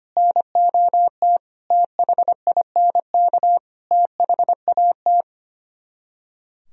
Morse code, heard as a keyed tone, about 24 wpm, 700 Hz. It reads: NOT THINK THAT
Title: Speed-Racing - Sets of 5 Words 25wpm